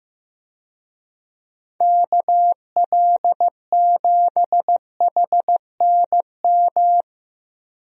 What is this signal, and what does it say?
Morse code, heard as a keyed tone, about 15 wpm, 700 Hz. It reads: KL7HNM
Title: USA Call Signs - 6 Characters - Repeated 3x - 15wpm